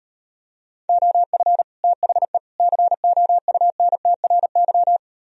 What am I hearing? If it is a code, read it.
Morse code, heard as a keyed tone, about 38 wpm, 700 Hz. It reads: OF THE COUNTRY